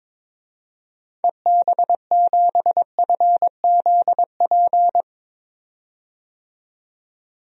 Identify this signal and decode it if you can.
Morse code, heard as a keyed tone, about 22 wpm, 700 Hz. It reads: EB7FZP
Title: DXCC-100 Call Signs - 6 Characters - 22wpm